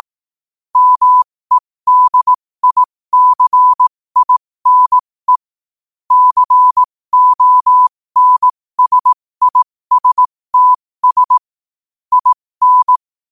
Morse code, heard as a keyed tone, about 18 wpm, spaced as for 15 wpm, 1000 Hz. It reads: MEDICINE CONSISTS IN
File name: qotdfiles / 15wpm